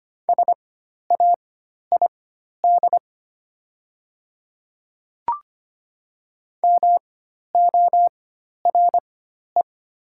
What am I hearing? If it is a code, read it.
Morse code, heard as a keyed tone, about 25 wpm, spaced as for 12 wpm, 700 Hz. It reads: SAID MORE
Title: 4x Character Spacing - Top 200 Words - 25wpm